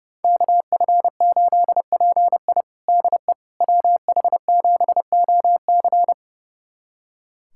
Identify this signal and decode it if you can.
Morse code, heard as a keyed tone, about 32 wpm, 700 Hz. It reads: KF8PI DE WH7OC
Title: USA Callsign DE Callsign - 5 Characters - 30wpm